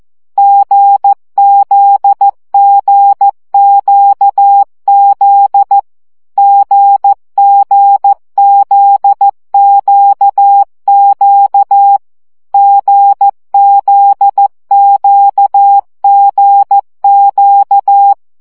Morse code, heard as a keyed tone, about 14 wpm, 800 Hz. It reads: GZGQZ GGZQQ GZQGQ
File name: CW gyakorlas g z q 12 baud